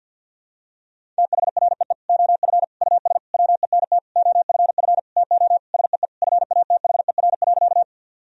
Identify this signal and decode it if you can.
Morse code, heard as a keyed tone, about 48 wpm, 700 Hz. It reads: THREE OF US WENT OFF TO SEE FATHER.
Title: Sentences from Top 1000 Words - Review Part 1 - 50wpm